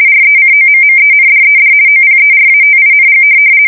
RTTY, about 50 baud. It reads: RECOMMENDED OPERATING
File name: fsk170-50B / teletype-corp